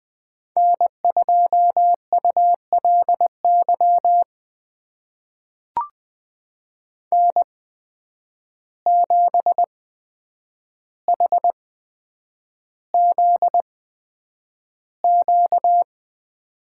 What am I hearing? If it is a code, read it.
Morse code, heard as a keyed tone, about 20 wpm, spaced as for 9 wpm, 700 Hz. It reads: N2ULY N7HZQ
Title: USA Call Signs - 5 Characters - 8x Character Spacing - 20wpm